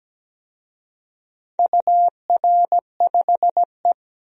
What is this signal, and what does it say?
Morse code, heard as a keyed tone, about 17 wpm, 700 Hz. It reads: UR5E
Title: DXCC-100 Call Signs - 4 Characters - 17wpm